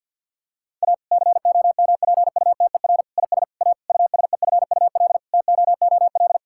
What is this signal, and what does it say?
Morse code, heard as a keyed tone, about 50 wpm, 700 Hz. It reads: A COMPUTER IS A USEFUL TOOL
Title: Sentences from Top 1000 Words - Review Part 6 - 50wpm